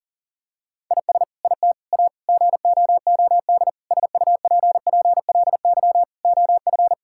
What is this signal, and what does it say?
Morse code, heard as a keyed tone, about 40 wpm, 700 Hz. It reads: IS IT A GOOD SUPPLY OF
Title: Sentences from Top 1000 Words - Review Part 5 - 40wpm